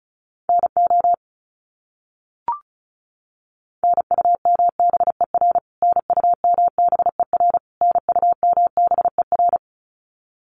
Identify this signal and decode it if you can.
Morse code, heard as a keyed tone, about 34 wpm, 700 Hz. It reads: NO NUMBER NUMBER NUMBER
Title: Top 300 Words Repeated 35wpm